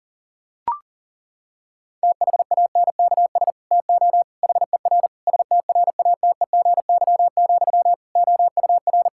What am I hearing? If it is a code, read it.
Morse code, heard as a keyed tone, about 40 wpm, 700 Hz. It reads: THANKS TO HER STRATEGY, OUR
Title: Sentences from Top 1000 Words - Review Part 5 - 40wpm